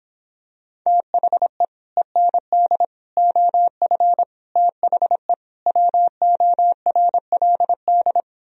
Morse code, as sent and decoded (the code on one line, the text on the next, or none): - .... / . -. -.. / --- ..-. / - .... / .-- --- .-. .-.. -..
THE END OF THE WORLD